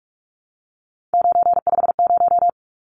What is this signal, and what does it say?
Morse code, heard as a keyed tone, about 44 wpm, 700 Hz. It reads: 950
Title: Numbers - 3-Digit 45wpm